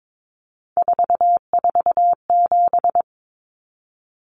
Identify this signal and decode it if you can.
Morse code, heard as a keyed tone, about 22 wpm, 700 Hz. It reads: 447
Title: Common Cut Numbers - 3-Digit 22wpm